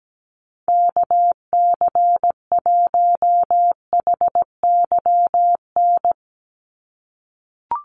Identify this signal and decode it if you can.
Morse code, heard as a keyed tone, about 17 wpm, 700 Hz. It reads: KC1HYN
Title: USA Call Signs 17wpm